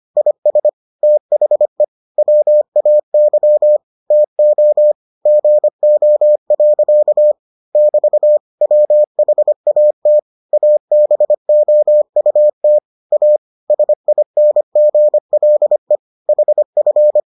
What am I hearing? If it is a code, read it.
Morse code, heard as a keyed tone, about 25 wpm, 600 Hz. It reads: IS THE WAY TO GO. <BT> WHAT ABOUT A SINGLE HF